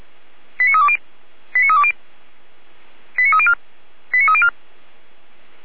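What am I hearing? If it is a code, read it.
Five-tone selective call (ZVEI), heard as quick runs of five tones, tones of about 80 ms. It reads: 89320 89320 89304 89304
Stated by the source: ZVEI five-tone selective call received on SDR.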